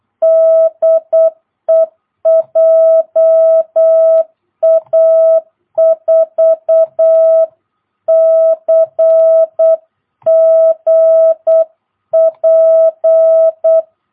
Morse code, heard as a keyed tone, about 8 wpm, 660 Hz. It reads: DE JA4CGP
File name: jh8cbh / 2022cw